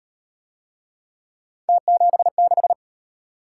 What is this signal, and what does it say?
Morse code, heard as a keyed tone, about 33 wpm, 700 Hz. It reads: T76